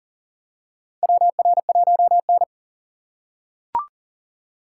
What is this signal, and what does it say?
Morse code, heard as a keyed tone, about 40 wpm, 700 Hz. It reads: WR1N
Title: USA Call Signs - 4 Characters - 3x Character Spacing - 40wpm